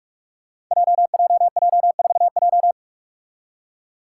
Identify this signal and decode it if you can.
Morse code, heard as a keyed tone, about 44 wpm, 700 Hz. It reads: JJJ4J
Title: Five Challenging Letter-Number Combinations 45wpm